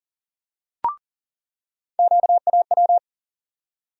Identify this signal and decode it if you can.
Morse code, heard as a keyed tone, about 40 wpm, 700 Hz. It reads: QAW
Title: USA Call Sign Postfixes - 3 Letters 40wpm